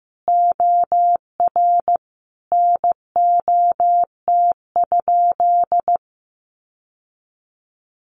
Morse code, heard as a keyed tone, about 15 wpm, 700 Hz. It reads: OR NOT?
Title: Sentences from Top 600 Words 15wpm